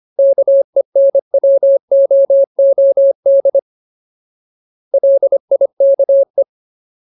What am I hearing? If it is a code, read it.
Morse code, heard as a keyed tone, about 25 wpm, 550 Hz. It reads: KENWOOD LIKE
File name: Ultimate Word List / 25WPM Ultimate